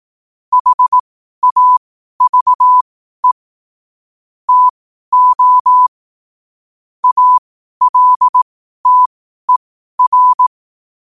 Morse code, heard as a keyed tone, about 18 wpm, spaced as for 12 wpm, 1000 Hz. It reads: HAVE TO ALTER
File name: qotdfiles / 12wpm